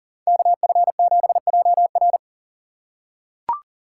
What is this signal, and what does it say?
Morse code, heard as a keyed tone, about 40 wpm, 700 Hz. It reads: KF7JR
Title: USA Call Signs - 5 Characters - 6x Character Spacing - 40wpm